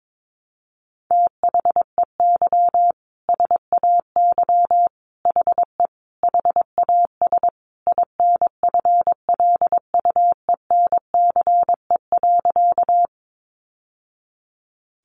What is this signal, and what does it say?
Morse code, heard as a keyed tone, about 22 wpm, 700 Hz. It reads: THEY SAY HE HAS INFLUENCE.
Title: Sentences from Top 1100 Words 22wpm